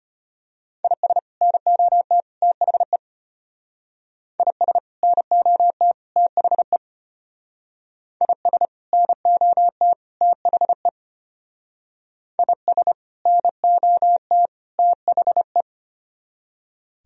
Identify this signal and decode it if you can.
Morse code, heard as a keyed tone, about 31 wpm, 700 Hz. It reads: IS NOT THE IS NOT THE IS NOT THE IS NOT THE
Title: Speed-Racing - Sets of 3 Words 25wpm